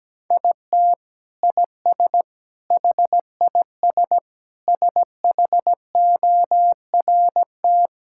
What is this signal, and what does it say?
Morse code, heard as a keyed tone, about 17 wpm, 700 Hz. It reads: IT IS HIS SHORT
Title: Sentences from Top 1600 Words 17wpm